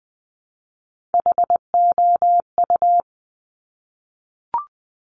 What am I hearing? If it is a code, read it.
Morse code, heard as a keyed tone, about 20 wpm, 700 Hz. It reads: HOU
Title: Top Three Letter Combinations in Words 20wpm